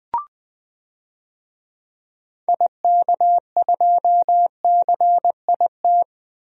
Morse code, heard as a keyed tone, about 20 wpm, 700 Hz. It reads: IK2CIT
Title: DXCC-100 Call Signs - 6 Characters - 20wpm